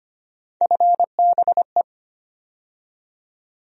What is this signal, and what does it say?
Morse code, heard as a keyed tone, about 25 wpm, 700 Hz. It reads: FBE
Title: USA Call Sign Postfixes - 3 Letters 25wpm